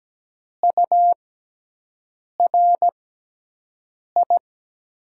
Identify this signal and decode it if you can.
Morse code, heard as a keyed tone, about 17 wpm, spaced as for 6 wpm, 700 Hz. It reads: URI
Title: USA Call Signs - 6 Characters - 6x Character Spacing - 17wpm